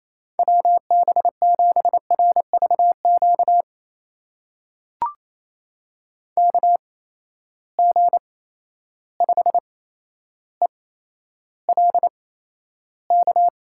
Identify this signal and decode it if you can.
Morse code, heard as a keyed tone, about 28 wpm, spaced as for 12 wpm, 700 Hz. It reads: WB7RVQ KG5ELK
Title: USA Call Signs - 6 Characters - 8x Character Spacing - 28wpm